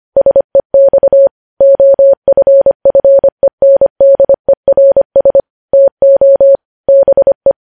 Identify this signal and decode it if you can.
Morse code, heard as a keyed tone, about 25 wpm, 550 Hz. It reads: SEX OFFENDERS TO BE
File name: podcast / News headlines-1x-25x25